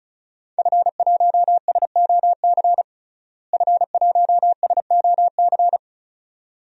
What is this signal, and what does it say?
Morse code, heard as a keyed tone, about 34 wpm, 700 Hz. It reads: F1SOC F1SOC
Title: DXCC-100 Call Signs - 5 Characters - Repeated 2x - 35wpm